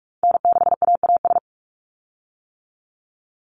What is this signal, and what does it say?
Morse code, heard as a keyed tone, about 44 wpm, 700 Hz. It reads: N6AAS
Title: USA Call Signs 45wpm